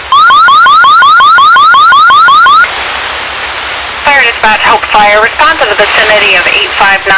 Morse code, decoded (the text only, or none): none